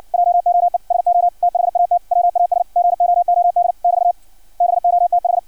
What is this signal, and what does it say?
Morse code, heard as a keyed tone, about 56 wpm, 700 Hz. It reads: JOE SO THAT WAS GOOD <BT> BOTH
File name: archived / CRSnet